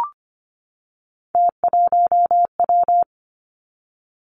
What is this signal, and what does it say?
Morse code, heard as a keyed tone, about 25 wpm, 700 Hz. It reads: T1W
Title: Three Challenging Letter-Number Combinations 25wpm